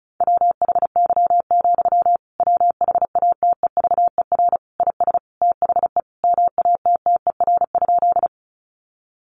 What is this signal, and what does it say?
Morse code, heard as a keyed tone, about 34 wpm, 700 Hz. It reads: WHY, WHATEVER IS THE MATTER?
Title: Sentences from Top 1100 Words 35wpm